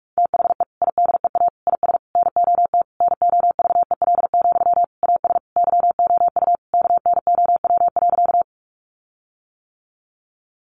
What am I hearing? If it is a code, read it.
Morse code, heard as a keyed tone, about 44 wpm, 700 Hz. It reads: THE IDEA IS NOT NOVEL, AS YOU KNOW.
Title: Sentences from Top 1300 Words 45wpm